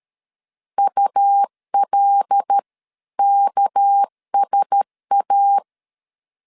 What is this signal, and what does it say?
Morse code, heard as a keyed tone, about 13 wpm, 800 Hz. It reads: ULKSA
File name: CW / DET / 1 9 1500C S A 12wpm 3800